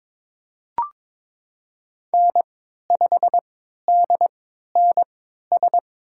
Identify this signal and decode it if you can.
Morse code, heard as a keyed tone, about 22 wpm, spaced as for 13 wpm, 700 Hz. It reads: N5DNS
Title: USA Call Signs - 3x Character Spacing - 22wpm